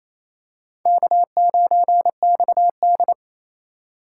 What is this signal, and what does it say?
Morse code, heard as a keyed tone, about 28 wpm, 700 Hz. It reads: K9XD
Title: K1USN SST QSO Practice - 28wpm